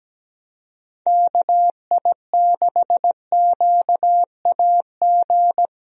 Morse code, heard as a keyed tone, about 17 wpm, 700 Hz. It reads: KI6QAG
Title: USA Call Signs - 6 Characters - Repeated 3x - 17wpm